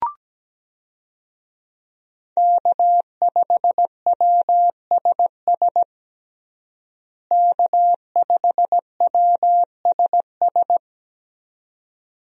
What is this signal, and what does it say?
Morse code, heard as a keyed tone, about 17 wpm, 700 Hz. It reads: K5WSS K5WSS
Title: USA Call Signs - 5 Characters - Repeated 2x - 17wpm